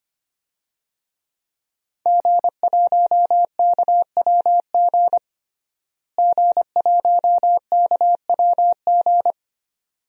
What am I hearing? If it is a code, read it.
Morse code, heard as a keyed tone, about 25 wpm, 700 Hz. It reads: G1KWG G1KWG
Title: DXCC-100 Call Signs - 5 Characters - Repeated 2x - 25wpm